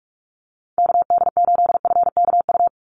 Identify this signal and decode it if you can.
Morse code, heard as a keyed tone, about 44 wpm, 700 Hz. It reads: KD8FKU
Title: USA Call Signs 45wpm